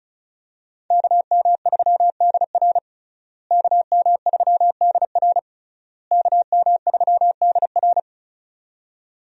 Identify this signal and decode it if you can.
Morse code, heard as a keyed tone, about 36 wpm, 700 Hz. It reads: KM3DR KM3DR KM3DR
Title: USA Call Signs - 5 Characters - Repeated 3x - 35wpm